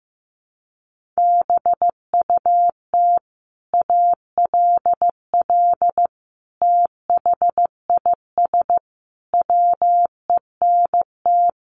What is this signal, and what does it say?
Morse code, heard as a keyed tone, about 15 wpm, 700 Hz. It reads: BUT ALL THIS WENT
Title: Sentences from Top 1500 Words 15wpm